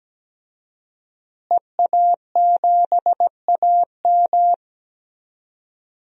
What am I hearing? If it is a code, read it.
Morse code, heard as a keyed tone, about 17 wpm, 700 Hz. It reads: EA7AM
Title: DXCC-100 Call Signs - 5 Characters - Repeated 2x - 17wpm